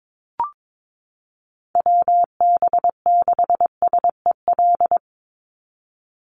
Morse code, heard as a keyed tone, about 22 wpm, 690 Hz. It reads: WB6SEL
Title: Call Signs Repeated 3x - 22wpm